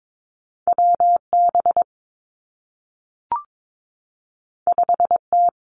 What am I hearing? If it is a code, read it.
Morse code, heard as a keyed tone, about 22 wpm, 700 Hz. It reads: W B 5 T
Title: Two Challenging Letter-Number Combinations 22wpm